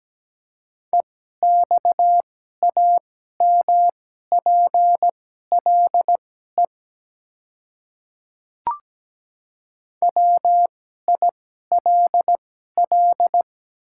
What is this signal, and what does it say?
Morse code, heard as a keyed tone, about 17 wpm, spaced as for 12 wpm, 700 Hz. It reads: EXAMPLE WILL
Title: 2x Character Spacing - Top 300 Words - 17wpm